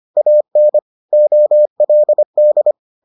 Morse code, heard as a keyed tone, about 25 wpm, 600 Hz. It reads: AN OLD